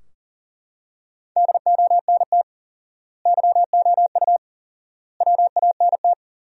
Morse code, heard as a keyed tone, about 40 wpm, 700 Hz. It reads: DONT YOU WANT
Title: 4x Word-Spacing - Sentences from Top 300 Words - 40wpm